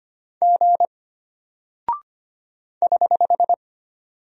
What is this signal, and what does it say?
Morse code, heard as a keyed tone, about 25 wpm, 700 Hz. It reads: G <HH>
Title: Single Character (Letter:Number:Punctuation:Prosign) Alt 25wpm